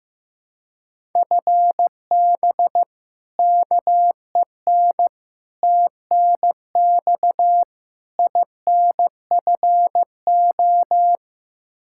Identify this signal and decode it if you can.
Morse code, heard as a keyed tone, about 15 wpm, 700 Hz. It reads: FB KEN TNX INFO
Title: Practice QSO - Exchange 3 - More Info - 15wpm